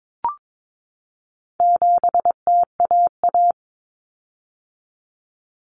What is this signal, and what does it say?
Morse code, heard as a keyed tone, about 22 wpm, 700 Hz. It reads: 7TAA
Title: Common Cut Numbers - 4-Digit 22wpm